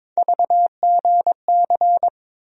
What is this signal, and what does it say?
Morse code, heard as a keyed tone, about 22 wpm, 700 Hz. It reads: VGC
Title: USA Call Sign Postfixes - 3 Letters 22wpm